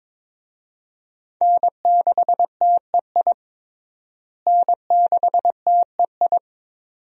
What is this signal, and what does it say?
Morse code, heard as a keyed tone, about 22 wpm, 700 Hz. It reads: N6TEI N6TEI
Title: USA Call Signs - 5 Characters - Repeated 2x - 22wpm